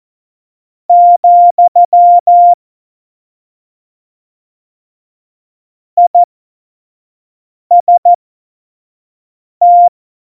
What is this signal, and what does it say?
Morse code, heard as a keyed tone, about 14 wpm, spaced as for 5 wpm, 700 Hz. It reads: , IST
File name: Morsekurs / Witze / Witz 27. November 2018, 25 F70